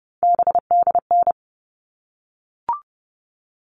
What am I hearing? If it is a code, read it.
Morse code, heard as a keyed tone, about 30 wpm, 700 Hz. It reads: BDN
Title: All Cut Numbers - 3-Digit 30wpm